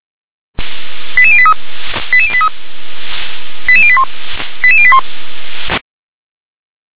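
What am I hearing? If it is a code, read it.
Five-tone selective call (ZVEI), heard as quick runs of five tones, tones of about 70 ms. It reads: 88083 88083 88081 88081
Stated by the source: ZVEI five-tone selective call received by SDR.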